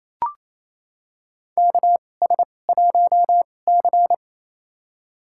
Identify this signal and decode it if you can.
Morse code, heard as a keyed tone, about 28 wpm, spaced as for 20 wpm, 700 Hz. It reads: KS1C
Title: USA Call Signs - 4 Characters - 2x Character Spacing - 28wpm